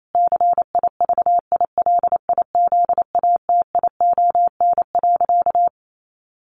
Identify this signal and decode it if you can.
Morse code, heard as a keyed tone, about 28 wpm, 700 Hz. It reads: CIVILIZATION.
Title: A Sound Of Thunder 28wpm